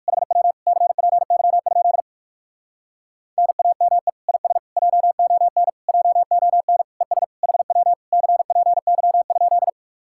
Morse code, heard as a keyed tone, about 59 wpm, 700 Hz. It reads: HW CPY? NAME IS JON JON ES HW CPY?